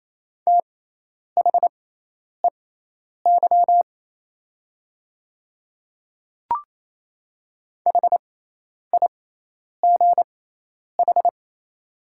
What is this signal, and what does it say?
Morse code, heard as a keyed tone, about 28 wpm, spaced as for 10 wpm, 700 Hz. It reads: THEY HIGH
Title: 6x Character Spacing - Top 200 Words - 28wpm